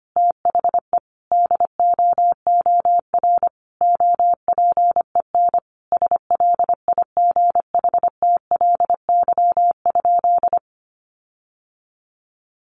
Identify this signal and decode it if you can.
Morse code, heard as a keyed tone, about 25 wpm, 700 Hz. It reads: THE DOOR OPEN SLIGHTLY?
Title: Sentences from Top 1300 Words 25wpm